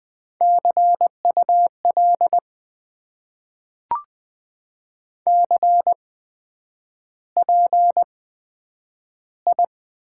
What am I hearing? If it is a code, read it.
Morse code, heard as a keyed tone, about 20 wpm, spaced as for 9 wpm, 700 Hz. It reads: CUL CPI
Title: Top 90 QSO Elements - 8x Character Spacing - 20wpm